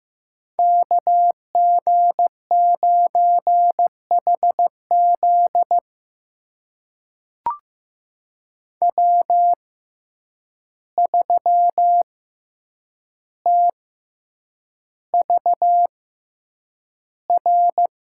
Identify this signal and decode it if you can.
Morse code, heard as a keyed tone, about 15 wpm, spaced as for 8 wpm, 700 Hz. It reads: KG9HZ W3TVR
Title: USA Call Signs - 5 Characters - 6x Character Spacing - 15wpm